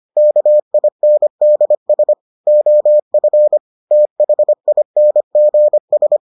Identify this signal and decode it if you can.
Morse code, heard as a keyed tone, about 25 wpm, 600 Hz. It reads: KINDS OF THINGS